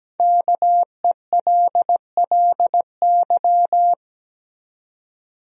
Morse code, heard as a keyed tone, about 17 wpm, 700 Hz. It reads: KELLY
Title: USA Common Names - Max 5 Letters - 17wpm